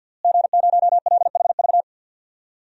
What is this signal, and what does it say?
Morse code, heard as a keyed tone, about 46 wpm, 700 Hz. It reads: G0LHV
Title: DXCC-100 Call Signs - 5 Characters - Repeated 3x - 50wpm